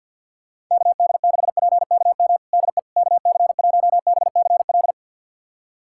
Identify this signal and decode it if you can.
Morse code, heard as a keyed tone, about 49 wpm, 700 Hz. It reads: KD6PKM DE KC1BCL
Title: USA Callsign DE Callsign 50wpm